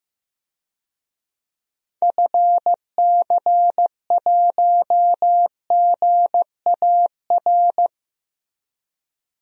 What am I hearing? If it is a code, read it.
Morse code, heard as a keyed tone, about 16 wpm, 700 Hz. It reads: FC1GAR
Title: DXCC-100 Call Signs - 6 Characters - Repeated 3x - 15wpm